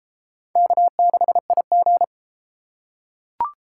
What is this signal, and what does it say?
Morse code, heard as a keyed tone, about 31 wpm, 700 Hz. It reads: K6IG